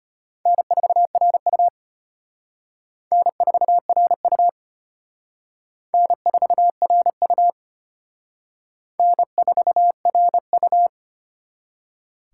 Morse code, heard as a keyed tone, about 31 wpm, 700 Hz. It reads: N4RU N4RU N4RU N4RU
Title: Speed Racing - USA 1x2 Call Signs 25wpm